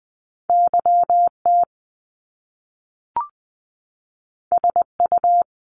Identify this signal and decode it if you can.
Morse code, heard as a keyed tone, about 20 wpm, 700 Hz. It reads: YT SU
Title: Top Two Letter Combinations in Words 20wpm